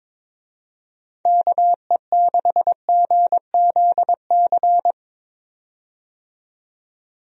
Morse code, heard as a keyed tone, about 22 wpm, 700 Hz. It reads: KE6GZC
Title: USA Call Signs - 6 Characters - 22wpm